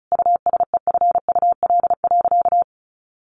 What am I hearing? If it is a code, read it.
Morse code, heard as a keyed tone, about 34 wpm, 700 Hz. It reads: USEFUL.
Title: Sentences from Top 1400 Words 35wpm